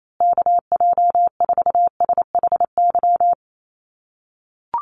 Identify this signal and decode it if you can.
Morse code, heard as a keyed tone, about 28 wpm, 700 Hz. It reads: KJ4SHY
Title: USA Call Signs Repeated 3x 28wpm